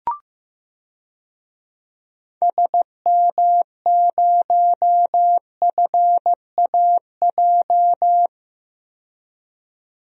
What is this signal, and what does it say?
Morse code, heard as a keyed tone, about 15 wpm, 700 Hz. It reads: SM0FAJ
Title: DXCC-100 Call Signs - 6 Characters - 15wpm